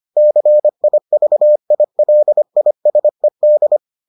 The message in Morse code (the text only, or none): CIVILISED